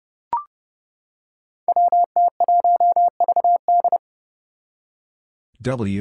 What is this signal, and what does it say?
Morse code, heard as a keyed tone, about 30 wpm, 700 Hz. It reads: WT1VD
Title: Five Challenging Letter-Number Combinations 30wpm